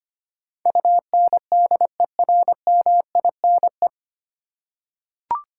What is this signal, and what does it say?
Morse code, heard as a keyed tone, about 25 wpm, 700 Hz. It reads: UNDERMINE
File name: Nine Letter Words - 25wpm